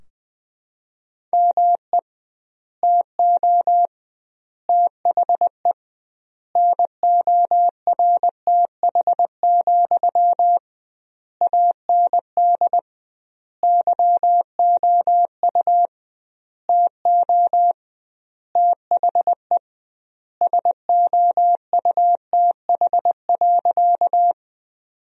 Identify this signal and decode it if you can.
Morse code, heard as a keyed tone, about 20 wpm, 700 Hz. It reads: ME TO THE NORTH, AND YOU TO THE SOUTH.
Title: 2x Word-Spacing - Sentences from Top 400 Words - 20wpm